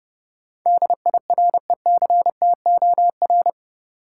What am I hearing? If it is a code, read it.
Morse code, heard as a keyed tone, about 30 wpm, 700 Hz. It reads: DIRECTOR
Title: Sentences from Top 1300 Words - New Word Drill Alt - 30wpm